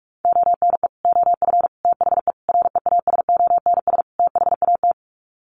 Morse code, heard as a keyed tone, about 44 wpm, 700 Hz. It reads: ONE OF THE REASONS THAT
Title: Sets of 5 Words Encore 45wpm